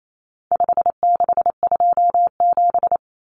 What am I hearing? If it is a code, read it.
Morse code, heard as a keyed tone, about 28 wpm, 700 Hz. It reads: 5627
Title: Common Cut Numbers - 4-Digit 28wpm